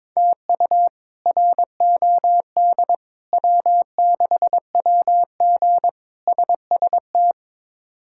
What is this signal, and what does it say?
Morse code, heard as a keyed tone, about 21 wpm, 700 Hz. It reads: TU ROD W6WG SST
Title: K1USN SST QSO Practice - 22wpm